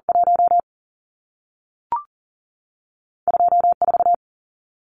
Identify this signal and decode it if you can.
Morse code, heard as a keyed tone, about 40 wpm, spaced as for 3 wpm, 700 Hz. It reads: 12 4
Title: Numbers - 2-Digit 40wpm